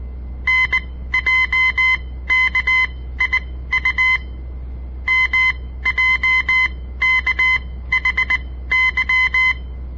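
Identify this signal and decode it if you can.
Morse code, heard as a keyed tone, about 19 wpm, spaced as for 14 wpm, 2050 Hz. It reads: NJKIU MJKHY
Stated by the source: On-air Morse code practice broadcast by a southern Montana amateur radio association; all code, no read back.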